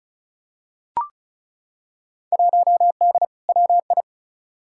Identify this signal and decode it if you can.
Morse code, heard as a keyed tone, about 36 wpm, 700 Hz. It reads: 1D WI
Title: Field Day Exchanges - 35wpm